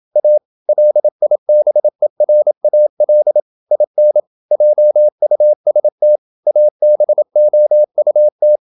Morse code, heard as a keyed tone, about 28 wpm, 600 Hz. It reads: A LIBERAL IN JUST ABOUT
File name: Never 0027wpm